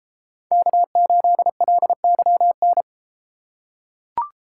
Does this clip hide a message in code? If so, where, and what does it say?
Yes: Morse code from 0.5 to 2.8 s, heard as a keyed tone, about 33 wpm, 700 Hz. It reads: K8LYN